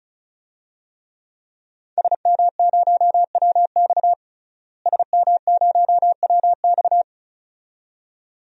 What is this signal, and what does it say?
Morse code, heard as a keyed tone, about 36 wpm, 700 Hz. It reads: SM0WX SM0WX
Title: DXCC-100 Call Signs - 5 Characters - Repeated 2x - 35wpm